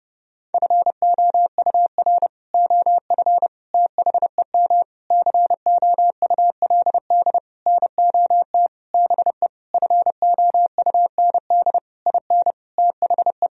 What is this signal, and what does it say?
Morse code, heard as a keyed tone, about 30 wpm, 700 Hz. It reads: FOUR OF THEM COULD NOT BE FOUND IN THE
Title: Sentences from Top 1000 Words - Review Part 1 - 30wpm